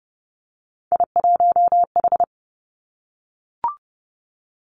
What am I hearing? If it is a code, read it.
Morse code, heard as a keyed tone, about 30 wpm, 700 Hz. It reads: I1H
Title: Three Challenging Letter-Number Combinations 30wpm